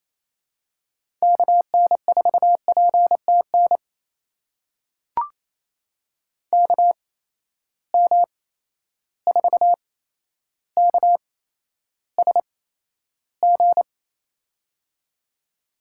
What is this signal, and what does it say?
Morse code, heard as a keyed tone, about 28 wpm, spaced as for 12 wpm, 700 Hz. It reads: KN4PTN KM4KSG
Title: USA Call Signs - 8x Character Spacing - 28wpm